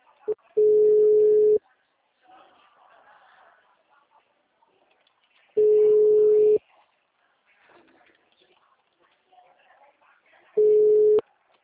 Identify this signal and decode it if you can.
Ringback tone (Europe, 425 Hz).